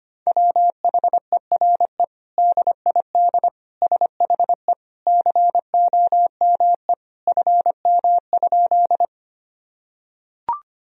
Morse code, heard as a keyed tone, about 25 wpm, 700 Hz. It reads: WHERE DID SHE COME FM?
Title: Sentences from Top 200 Words and Abbreviations 25wpm